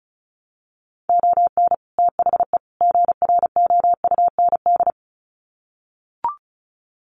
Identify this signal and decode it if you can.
Morse code, heard as a keyed tone, about 34 wpm, 700 Hz. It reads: ON THE GROUND
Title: Sets of 3 Words 35wpm